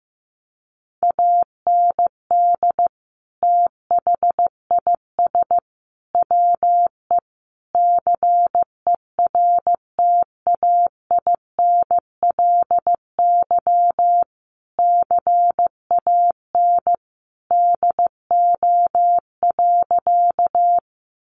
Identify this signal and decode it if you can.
Morse code, heard as a keyed tone, about 15 wpm, 700 Hz. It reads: AND THIS WE CERTAINLY CAN DO.
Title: Sentences from Top 900 Words 15wpm